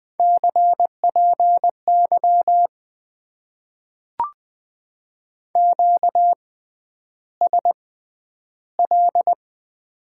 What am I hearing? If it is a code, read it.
Morse code, heard as a keyed tone, about 20 wpm, spaced as for 10 wpm, 700 Hz. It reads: CPY QSL
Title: Top 90 QSO Elements - 6x Character Spacing - 20wpm